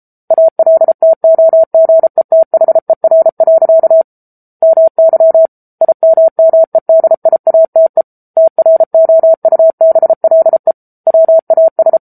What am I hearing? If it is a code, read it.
Morse code, heard as a keyed tone, about 34 wpm, 650 Hz. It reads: ALTOGETHER. MY IMMEDIATE TROUBLE WAS
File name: Book Two / 07-Chapter 7 - 33 WPM